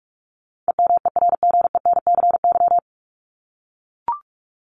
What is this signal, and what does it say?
Morse code, heard as a keyed tone, about 44 wpm, 700 Hz. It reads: EMERGENCY
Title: Top 1501-1750 Words 45wpm